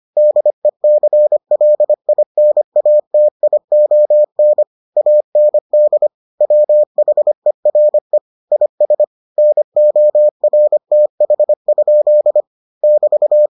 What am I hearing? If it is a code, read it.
Morse code, heard as a keyed tone, about 25 wpm, 600 Hz. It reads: DECLINATION AND WHERE IS NORTH? <BT>